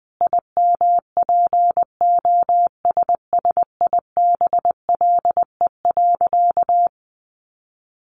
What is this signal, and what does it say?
Morse code, heard as a keyed tone, about 20 wpm, 700 Hz. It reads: IMPOSSIBLE.
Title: Sentences from Top 1100 Words 20wpm